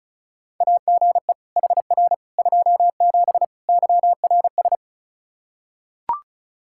Morse code, heard as a keyed tone, about 36 wpm, 700 Hz. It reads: AGE HR 27 YRS
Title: Practice QSO - Exchange 3 - More Info - 35wpm